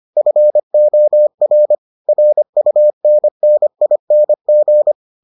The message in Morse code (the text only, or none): FOR RUNNING